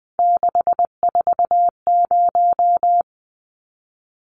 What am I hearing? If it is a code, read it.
Morse code, heard as a keyed tone, about 20 wpm, 700 Hz. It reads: 640
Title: Numbers - 3-Digit 20wpm